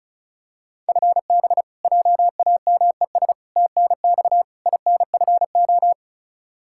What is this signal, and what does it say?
Morse code, heard as a keyed tone, about 36 wpm, 700 Hz. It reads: FB JAMES TNX INFO
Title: Practice QSO - Exchange 3 - More Info - 35wpm